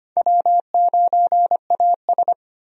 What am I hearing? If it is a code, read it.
Morse code, heard as a keyed tone, about 25 wpm, 700 Hz. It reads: W9AS
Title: USA Call Signs - 4 Characters - Repeated 3x - 25wpm